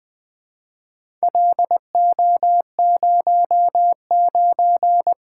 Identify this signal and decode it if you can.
Morse code, heard as a keyed tone, about 20 wpm, 700 Hz. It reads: LO09
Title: Four Letter-Number Combinations - 20wpm